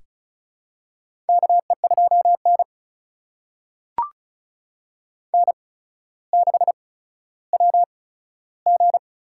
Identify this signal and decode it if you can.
Morse code, heard as a keyed tone, about 36 wpm, spaced as for 15 wpm, 700 Hz. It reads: KE2N N6WG
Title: USA Call Signs - 4 Characters - 8x Character Spacing - 35wpm